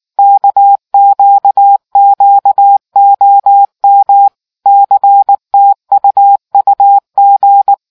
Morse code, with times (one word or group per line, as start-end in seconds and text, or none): 0.2-4.3 KQQOM
4.7-7.7 CTUUG